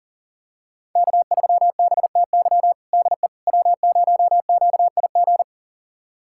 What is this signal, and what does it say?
Morse code, heard as a keyed tone, about 40 wpm, 700 Hz. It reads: K3BTY DE W0QIG
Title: USA Callsign DE Callsign - 5 Characters - 40wpm